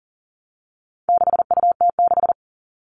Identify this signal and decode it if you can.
Morse code, heard as a keyed tone, about 40 wpm, 700 Hz. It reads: 6UT6
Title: All Cut Numbers - 4-Digit 40wpm